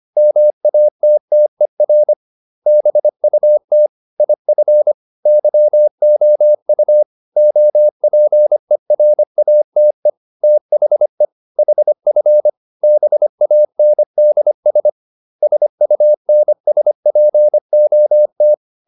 Morse code, heard as a keyed tone, about 25 wpm, 600 Hz. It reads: MATTER BUT IF YOU OPERATE THE HF BANDS SUNSPOT